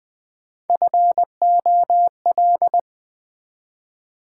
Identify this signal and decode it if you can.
Morse code, heard as a keyed tone, about 20 wpm, 690 Hz. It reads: FOL
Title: Build A Word - 1 Letter At A Time - 4-Letter Words - 20wpm